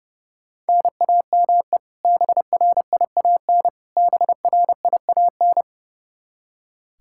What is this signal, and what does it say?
Morse code, heard as a keyed tone, about 30 wpm, 700 Hz. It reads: NAME BRIAN BRIAN
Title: K1USN SST QSO Practice 30wpm